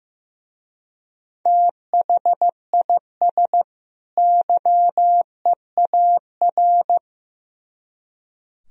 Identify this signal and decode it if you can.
Morse code, heard as a keyed tone, about 17 wpm, 700 Hz. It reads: THIS YEAR
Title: Speed-Racing - Sets of 2 Words 15wpm